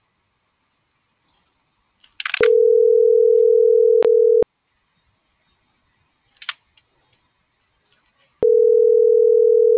Ringback tone (North America), 440 + 480 Hz.